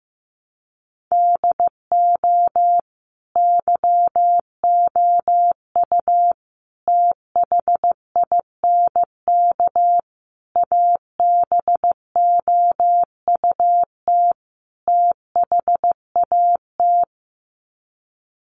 Morse code, heard as a keyed tone, about 15 wpm, 700 Hz. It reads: DO YOU THINK ABOUT THAT
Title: Sets of 5 Words Encore2 15wpm